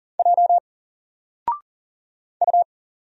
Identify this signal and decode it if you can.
Morse code, heard as a keyed tone, about 40 wpm, 700 Hz. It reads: J U
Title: Single Letter-Number Alt 40wpm